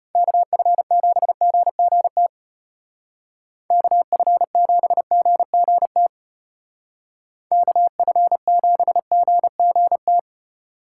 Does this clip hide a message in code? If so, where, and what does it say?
Yes: Morse code from 0.1 to 10.2 s, heard as a keyed tone, about 33 wpm, 700 Hz. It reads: KF7GGT KF7GGT KF7GGT